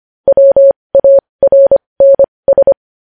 Morse code, heard as a keyed tone, about 25 wpm, spaced as for 20 wpm, 550 Hz. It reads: WARNS
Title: News Headlines at 20